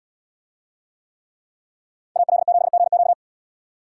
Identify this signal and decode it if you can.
Morse code, heard as a keyed tone, about 73 wpm, 700 Hz. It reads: AL7GQ